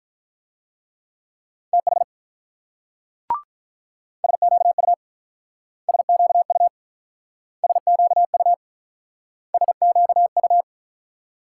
Morse code, heard as a keyed tone, about 48 wpm, 700 Hz. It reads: TH SQU SQU SQU SQU